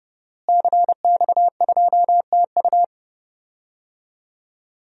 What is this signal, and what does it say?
Morse code, heard as a keyed tone, about 30 wpm, 700 Hz. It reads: CX2TU
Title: DXCC-100 Call Signs - 5 Characters - Repeated 2x - 30wpm